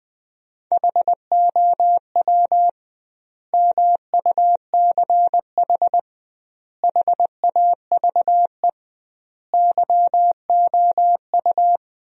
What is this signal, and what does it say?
Morse code, heard as a keyed tone, about 21 wpm, 700 Hz. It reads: HOW MUCH HAVE YOU
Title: 2x Word-Spacing - Sentences from Top 300 Words - 20wpm